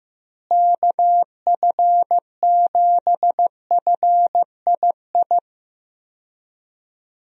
Morse code, heard as a keyed tone, about 16 wpm, 700 Hz. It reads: KF7FII
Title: USA Call Signs - 6 Characters - Repeated 2x - 15wpm